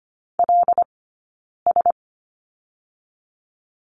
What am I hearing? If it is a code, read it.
Morse code, heard as a keyed tone, about 25 wpm, 700 Hz. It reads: L S
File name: Lesson 15d - Number 5 - Two Characters 25wpm